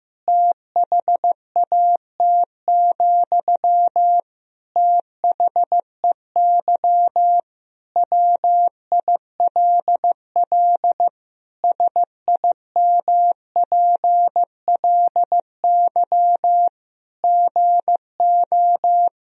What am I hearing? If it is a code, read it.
Morse code, heard as a keyed tone, about 15 wpm, 700 Hz. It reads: THAT, THEY WILL SIMPLY GO
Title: Sentences from Top 1000 Words - Review Part 4 - 15wpm